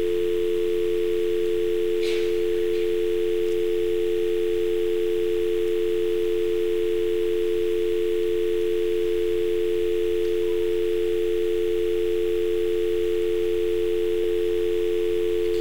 Dial tone (North America), 350 + 440 Hz.